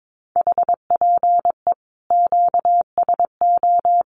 Morse code, heard as a keyed tone, about 22 wpm, 700 Hz. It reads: HPE QSO
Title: Common QSO Phrases 22wpm